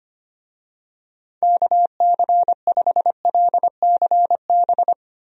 Morse code, heard as a keyed tone, about 25 wpm, 700 Hz. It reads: KC5LCB
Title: USA Call Signs - 6 Characters - Repeated 2x - 25wpm